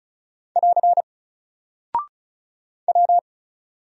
Morse code, heard as a keyed tone, about 36 wpm, 700 Hz. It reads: <AR> W
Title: Single Character (Letter:Number:Punctuation:Prosign) Alt 35wpm